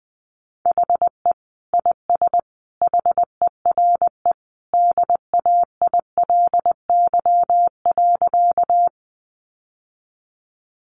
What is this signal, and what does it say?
Morse code, heard as a keyed tone, about 20 wpm, 700 Hz. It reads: HE IS HERE DAILY.
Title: Sentences from Top 1100 Words 20wpm